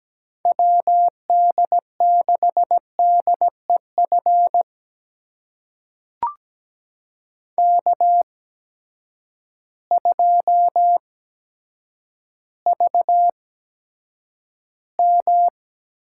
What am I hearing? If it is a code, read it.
Morse code, heard as a keyed tone, about 17 wpm, spaced as for 9 wpm, 700 Hz. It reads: WD6DEF K2VM
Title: USA Call Signs - 8x Character Spacing - 17wpm